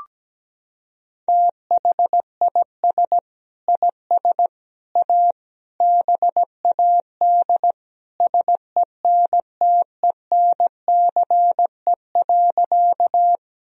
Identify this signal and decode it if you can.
Morse code, heard as a keyed tone, about 17 wpm, 700 Hz. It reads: THIS IS A BAD SENTENCE.
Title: Sentences from Top 1000 Words - Review Part 1 - 17wpm